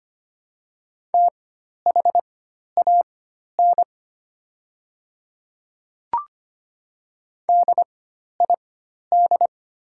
Morse code, heard as a keyed tone, about 25 wpm, spaced as for 12 wpm, 700 Hz. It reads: THAN DID
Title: 4x Character Spacing - Top 300 Words - 25wpm